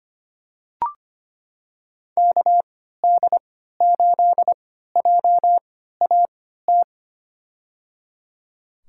Morse code, heard as a keyed tone, about 25 wpm, spaced as for 14 wpm, 700 Hz. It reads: KD8JAT
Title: USA Call Signs - 6 Characters - 3x Character Spacing - 25wpm